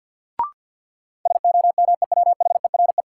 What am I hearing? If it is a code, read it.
Morse code, heard as a keyed tone, about 47 wpm, 700 Hz. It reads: SOMEWHERE
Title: Sentences from Top 1400 Words - New Word Drill Alt - 50wpm